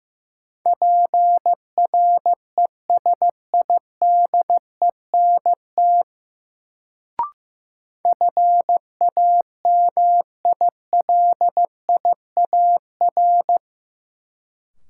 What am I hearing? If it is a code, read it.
Morse code, heard as a keyed tone, about 15 wpm, 700 Hz. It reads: PRESIDENT FAMILIAR
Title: Sentences from Top 1000 Words - New Word Drill - 15wpm